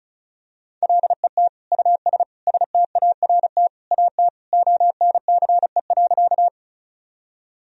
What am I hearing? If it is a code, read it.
Morse code, heard as a keyed tone, about 34 wpm, 700 Hz. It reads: LET US START AT ONCE.
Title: Sentences from Top 1000 Words - Review Part 1 - 35wpm